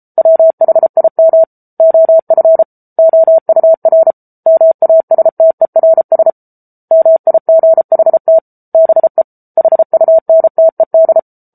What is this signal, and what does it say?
Morse code, heard as a keyed tone, about 34 wpm, 650 Hz. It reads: WHIM OF OUR MASTERS MIGHT BE HUNTED
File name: Book Two / 07-Chapter 7 - 33 WPM